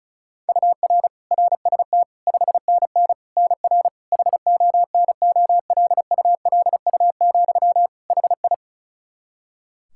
Morse code, heard as a keyed tone, about 36 wpm, 700 Hz. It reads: UR RST 5NN NR HONOLULU, HI
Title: Practice QSO - Exchange 1 - Beginning - 35wpm